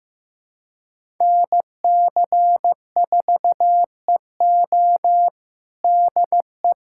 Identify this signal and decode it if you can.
Morse code, heard as a keyed tone, about 15 wpm, 700 Hz. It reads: NC4EO DE
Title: USA Callsign DE Callsign 15wpm